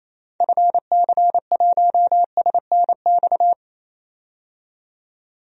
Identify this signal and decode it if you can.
Morse code, heard as a keyed tone, about 27 wpm, 700 Hz. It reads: FC1SNX
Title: DXCC-100 Call Signs - 6 Characters - 28wpm